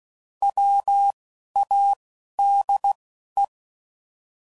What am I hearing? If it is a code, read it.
Morse code, heard as a keyed tone, about 16 wpm, spaced as for 11 wpm, 800 Hz. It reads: WADE
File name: Prefixes / By First Letter / Ws